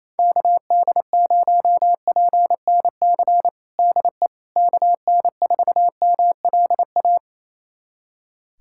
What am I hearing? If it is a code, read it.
Morse code, heard as a keyed tone, about 28 wpm, 700 Hz. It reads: KD0PNC DE KN4MLA
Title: USA Callsign DE Callsign - 6 Characters - 28wpm